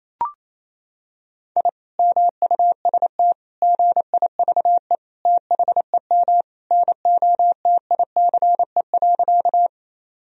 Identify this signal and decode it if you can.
Morse code, heard as a keyed tone, about 28 wpm, 700 Hz. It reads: I MUST GIVE THEM NOTICE.
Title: Sentences from Top 1000 Words - Review Part 2 - 28wpm